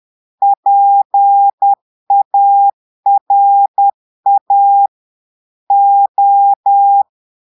Morse code, heard as a keyed tone, about 10 wpm, 800 Hz. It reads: PARA O
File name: Portugais - texte en clair 1 - vitesse - 10 mots minute